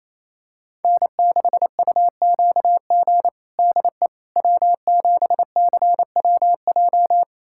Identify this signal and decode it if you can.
Morse code, heard as a keyed tone, about 28 wpm, 700 Hz. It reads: N6UQG DE W7CWJ
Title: USA Callsign DE Callsign - 5 Characters - 28wpm